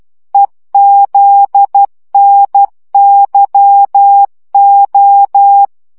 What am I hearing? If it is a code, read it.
Morse code, heard as a keyed tone, about 12 wpm, 800 Hz. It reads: EZNYO
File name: morze vizsga 2011-3 10